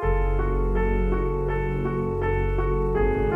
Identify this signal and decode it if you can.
There is no code.